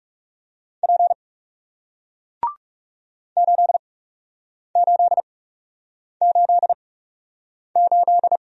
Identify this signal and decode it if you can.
Morse code, heard as a keyed tone, about 40 wpm, spaced as for 8 wpm, 700 Hz. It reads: P 8888